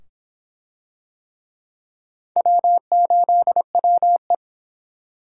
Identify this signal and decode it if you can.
Morse code, heard as a keyed tone, about 23 wpm, 700 Hz. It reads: W8WE